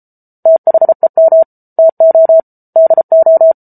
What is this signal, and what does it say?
Morse code, heard as a keyed tone, about 34 wpm, 650 Hz. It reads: THEM TO DO